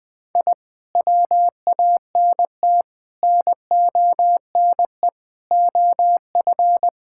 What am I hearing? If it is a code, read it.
Morse code, heard as a keyed tone, about 20 wpm, 700 Hz. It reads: I WANT NONE OF
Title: Sentences with 2-Word Phrases - 20wpm